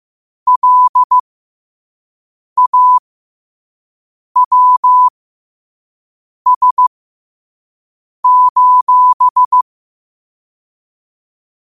Morse code, heard as a keyed tone, about 15 wpm, spaced as for 5 wpm, 1000 Hz. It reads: LAWS:
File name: qotdfiles / 5wpm